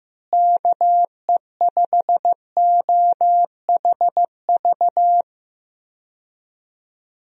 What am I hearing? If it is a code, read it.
Morse code, heard as a keyed tone, about 16 wpm, 700 Hz. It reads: KE5OHV
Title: USA Call Signs - 6 Characters - Repeated 3x - 15wpm